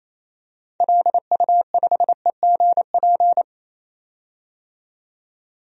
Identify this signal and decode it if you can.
Morse code, heard as a keyed tone, about 27 wpm, 700 Hz. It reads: LU5EGP
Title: DXCC-100 Call Signs - 6 Characters - 28wpm